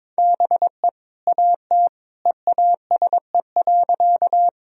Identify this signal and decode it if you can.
Morse code, heard as a keyed tone, about 22 wpm, 700 Hz. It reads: BE AT EASE.
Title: Sentences from Top 1000 Words - Review Part 1 - 22wpm